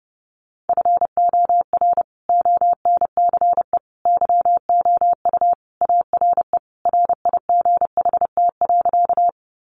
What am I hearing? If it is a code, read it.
Morse code, heard as a keyed tone, about 30 wpm, 700 Hz. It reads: FOR ONCE YOU ARE RIGHT.
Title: Sentences from Top 300 Words 30wpm